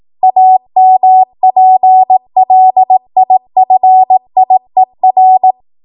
Morse code, heard as a keyed tone, about 18 wpm, 750 Hz. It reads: AMPLIFIER
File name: Morse / 18WPM